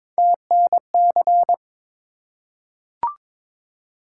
Morse code, heard as a keyed tone, about 22 wpm, 700 Hz. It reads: TNC
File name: practice / 22wpm / Three Letter-Number Combinations - 22wpm